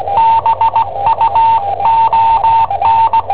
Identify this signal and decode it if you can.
Morse code, heard as a keyed tone, about 18 wpm, 900 Hz, with radio noise. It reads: BUON
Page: Cassetta esercizi preparata da I7OHP per IW5ELP